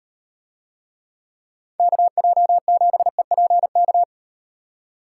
Morse code, heard as a keyed tone, about 38 wpm, 700 Hz. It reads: KJ7EPK